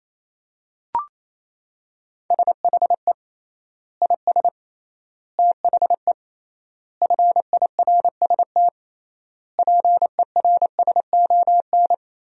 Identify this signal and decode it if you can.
Morse code, heard as a keyed tone, about 28 wpm, 700 Hz. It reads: SHE IS THE FIRST PERSON
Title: 3x Word-Spacing - Sentences from Top 400 Words - 28wpm